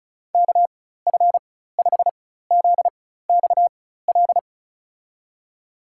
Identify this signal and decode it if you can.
Morse code, heard as a keyed tone, about 36 wpm, spaced as for 16 wpm, 700 Hz. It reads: KF5ZXL
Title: USA Call Signs - 6 Characters - 4x Character Spacing - 35wpm